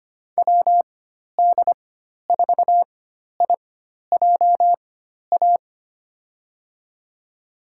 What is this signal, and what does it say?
Morse code, heard as a keyed tone, about 25 wpm, spaced as for 12 wpm, 700 Hz. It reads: WD4IJA
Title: USA Call Signs - 6 Characters - 4x Character Spacing - 25wpm